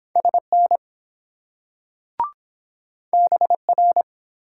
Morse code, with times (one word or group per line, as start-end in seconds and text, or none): 0.2-0.8 SN
3.1-4.0 BR